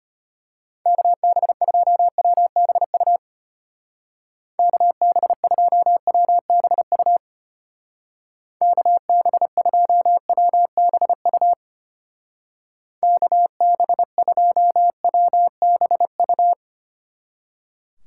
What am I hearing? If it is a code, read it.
Morse code, heard as a keyed tone, about 32 wpm, 700 Hz. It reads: KB2WBU KB2WBU KB2WBU KB2WBU